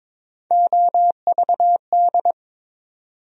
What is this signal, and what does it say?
Morse code, heard as a keyed tone, about 22 wpm, 700 Hz. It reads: OVD
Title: USA Call Sign Postfixes - 3 Letters 22wpm